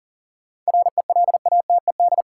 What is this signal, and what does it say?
Morse code, heard as a keyed tone, about 40 wpm, 700 Hz. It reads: RELATED
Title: Sentences from Top 1000 Words - Review Part 5 - New Word Drill - 40wpm